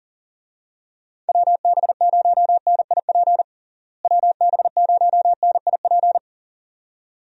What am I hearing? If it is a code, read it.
Morse code, heard as a keyed tone, about 40 wpm, 700 Hz. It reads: WB0NIP WB0NIP
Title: USA Call Signs - 6 Characters - Repeated 2x - 40wpm